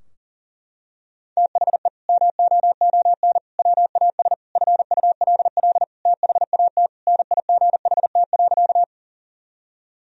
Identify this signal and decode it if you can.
Morse code, heard as a keyed tone, about 40 wpm, 700 Hz. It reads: THE MOON WAS FULL THAT NIGHT.
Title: Sentences from Top 1000 Words - Review Part 3 - 40wpm